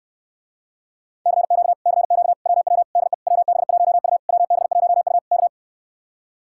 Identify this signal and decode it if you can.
Morse code, heard as a keyed tone, about 69 wpm, 700 Hz. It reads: CQ CQ WW DE WB1U WB1U K